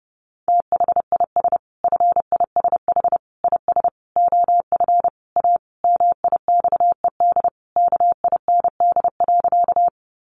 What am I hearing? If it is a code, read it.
Morse code, heard as a keyed tone, about 30 wpm, 700 Hz. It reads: THIS FISH IS OF A MIXED KIND.
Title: Sentences from Top 1500 Words 30wpm